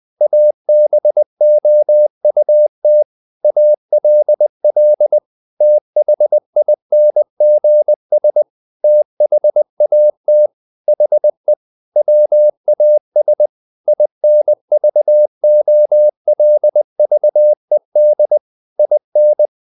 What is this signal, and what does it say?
Morse code, heard as a keyed tone, about 20 wpm, 600 Hz. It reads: ABOUT ALL THINGS THAT HE WAS INVOLVED IN
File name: the old man and the sea part 2 20wpm